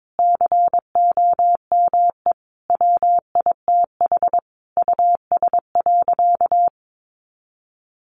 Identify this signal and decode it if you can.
Morse code, heard as a keyed tone, about 22 wpm, 700 Hz. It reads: COME WITH US.
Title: Sentences from Top 1300 Words 22wpm